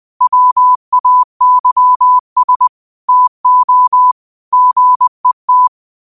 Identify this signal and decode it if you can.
Morse code, heard as a keyed tone, about 20 wpm, 1000 Hz. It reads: WAYS TO GET
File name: qotdfiles / 20wpm